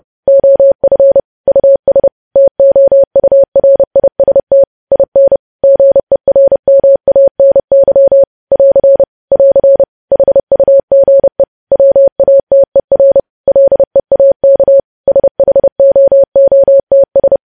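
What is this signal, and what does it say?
Morse code, heard as a keyed tone, about 30 wpm, 550 Hz. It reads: OF US TOURIST IN GERMANY <AR> <AR> HUGE WATER LEAK SHOOTS
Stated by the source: Morse code news headlines sent at 30.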